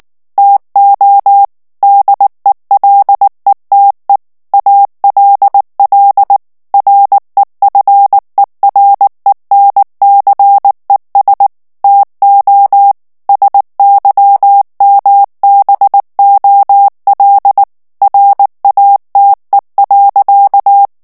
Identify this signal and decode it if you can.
Morse code, heard as a keyed tone, about 19 wpm, 800 Hz. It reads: TO DELETE ALL REFERENCES TO SYMBOL RATE.
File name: ANGLAIS texte clair 3 vitesse 18 mots minute